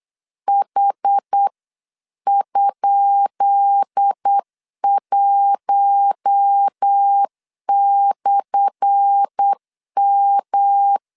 Morse code, heard as a keyed tone, about 8 wpm, 800 Hz. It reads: H?1/M
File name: LEZIONI / 10 SIGNS / 10B INT 8wpm 3534